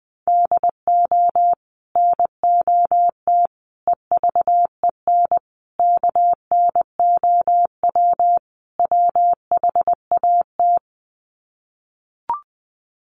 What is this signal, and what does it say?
Morse code, heard as a keyed tone, about 19 wpm, 700 Hz. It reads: DO NOT EVEN KNOW WHAT
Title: Sets of 5 Words Encore 20wpm